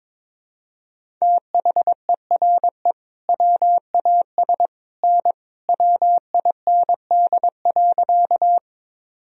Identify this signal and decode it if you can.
Morse code, heard as a keyed tone, about 22 wpm, 700 Hz. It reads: THERE WAS N WIND.
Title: Sentences from Top 400 Words and Abbreviations 22wpm